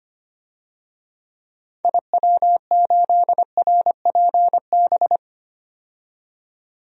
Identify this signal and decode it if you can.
Morse code, heard as a keyed tone, about 25 wpm, 700 Hz. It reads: IW8RPB
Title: DXCC-100 Call Signs - 6 Characters - 25wpm